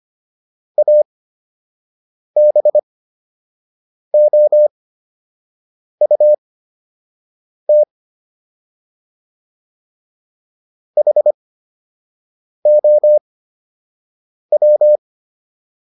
Morse code, heard as a keyed tone, about 25 wpm, spaced as for 6 wpm, 600 Hz. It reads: ABOUT HOW